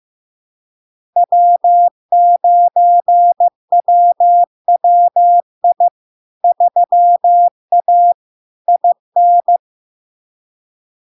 Morse code, heard as a keyed tone, about 15 wpm, 700 Hz. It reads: W9WWI 3A IN
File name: LARK-FD-2022-Log-in-CW-15wpm-700-Hz